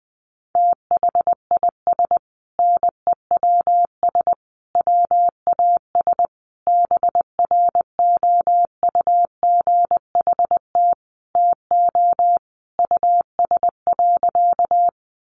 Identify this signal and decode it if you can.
Morse code, heard as a keyed tone, about 20 wpm, 700 Hz. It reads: THIS NEWS WAS BROUGHT TO US.
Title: Sentences from Top 800 Words 20wpm